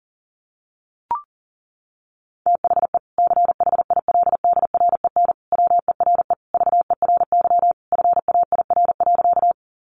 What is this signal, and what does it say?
Morse code, heard as a keyed tone, about 40 wpm, 700 Hz. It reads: THE CHILDREN WERE VERY FAIR.
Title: Sentences from Top 1300 Words 40wpm